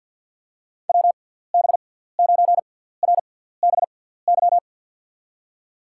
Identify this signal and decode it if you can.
Morse code, heard as a keyed tone, about 53 wpm, spaced as for 17 wpm, 700 Hz. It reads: WB9RBY
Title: USA Call Signs - 6 Characters - 6x Character Spacing - 50wpm